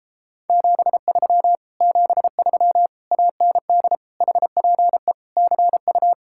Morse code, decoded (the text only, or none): 73 73 AND HPE CU